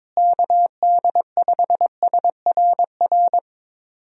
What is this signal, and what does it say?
Morse code, heard as a keyed tone, about 22 wpm, 700 Hz. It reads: KD5SRR
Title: USA Call Signs - 6 Characters - Repeated 3x - 22wpm